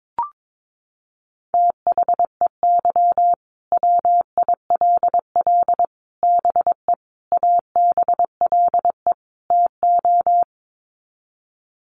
Morse code, heard as a keyed tone, about 22 wpm, 700 Hz. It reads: THEY WILL BE ABLE TO
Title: Sets of 5 Words Encore 22wpm